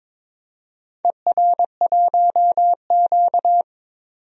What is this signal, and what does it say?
Morse code, heard as a keyed tone, about 22 wpm, 700 Hz. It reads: ER1Q
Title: DXCC-100 Call Signs - 4 Characters - 22wpm